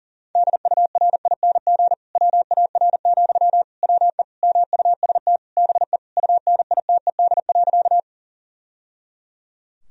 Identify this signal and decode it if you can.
Morse code, heard as a keyed tone, about 40 wpm, 700 Hz. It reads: DURING WAR, WE MUST BE UNITED.
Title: Sentences from Top 1000 Words - Review Part 4 - 40wpm